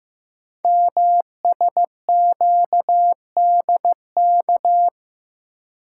Morse code, heard as a keyed tone, about 15 wpm, 700 Hz. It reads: MSQDK
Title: Code Groups - 5 Letters - 15wpm